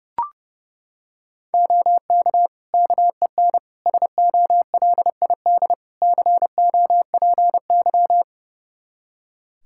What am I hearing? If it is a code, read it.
Morse code, heard as a keyed tone, about 32 wpm, 700 Hz. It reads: OK KEN SOLID COPY
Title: Practice QSO - Exchange 3 - More Info - 30wpm